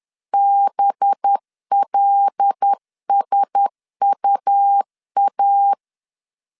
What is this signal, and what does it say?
Morse code, heard as a keyed tone, about 10 wpm, 800 Hz. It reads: BLSUA
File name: LEZIONI / 7 GF9Z / 7 GF9Z 10wpm 2611